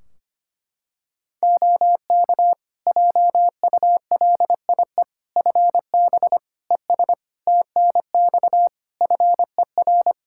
Morse code, heard as a keyed tone, about 25 wpm, 700 Hz. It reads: OK JULIE FB ES TNX FER
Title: Practice QSO - Exchange 2 - Info - 25wpm